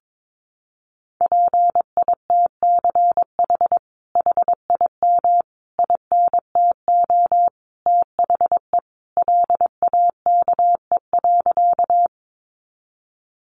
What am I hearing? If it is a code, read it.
Morse code, heard as a keyed tone, about 22 wpm, 700 Hz. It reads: PITCH HIM INTO THE LAKE.
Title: Sentences from Top 1300 Words 22wpm